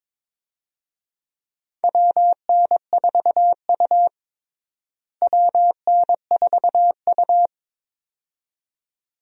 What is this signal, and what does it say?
Morse code, heard as a keyed tone, about 21 wpm, 700 Hz. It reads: WN4U WN4U
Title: USA Call Signs - 4 Characters - Repeated 2x - 22wpm